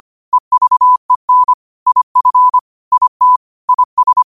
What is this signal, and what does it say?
Morse code, heard as a keyed tone, about 25 wpm, 1000 Hz. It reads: EVEN IF IT IS
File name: qotdfiles / 25wpm